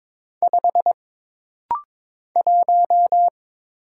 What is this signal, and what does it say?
Morse code, heard as a keyed tone, about 22 wpm, 700 Hz. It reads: 5 1
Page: This rapid-fire timing is geared to push your ICR (Instant Character Recognition) skills.